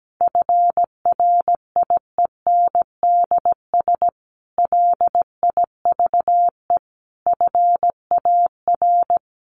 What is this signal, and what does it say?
Morse code, heard as a keyed tone, about 17 wpm, 700 Hz. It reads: FRIENDS LIVE FAR
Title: Additional Easy Phrases 17wpm